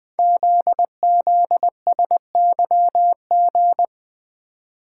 Morse code, heard as a keyed tone, about 20 wpm, 700 Hz. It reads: ZZSYG
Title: Code Groups - 5 Letters - 20wpm